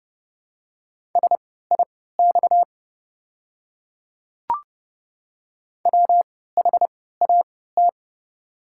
Morse code, heard as a keyed tone, about 29 wpm, spaced as for 17 wpm, 700 Hz. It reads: SIX WHAT
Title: 3x Character Spacing - Top 500 Words - 30wpm